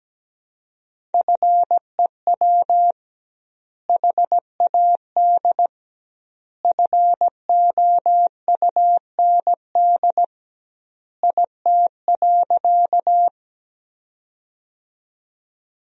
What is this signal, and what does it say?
Morse code, heard as a keyed tone, about 17 wpm, 700 Hz. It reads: FEW HAD FOUND IT.
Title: 2x Word-Spacing - Sentences from Top 300 Words - 17wpm